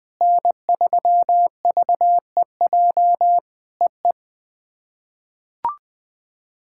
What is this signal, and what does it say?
Morse code, heard as a keyed tone, about 20 wpm, 700 Hz. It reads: N3VEJ EE
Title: POTA QSO Practice - 20wpm